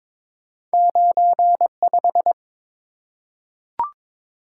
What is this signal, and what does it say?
Morse code, heard as a keyed tone, about 22 wpm, 700 Hz. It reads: 95
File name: practice / 22wpm / Two Letter-Number Combinations - 22wpm